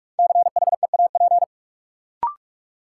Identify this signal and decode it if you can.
Morse code, heard as a keyed tone, about 46 wpm, 700 Hz. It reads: CHEAP
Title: Sentences from Top 1200 Words - New Word Drill Alt - 45wpm